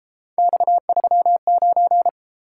Morse code, heard as a keyed tone, about 32 wpm, 700 Hz. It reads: X39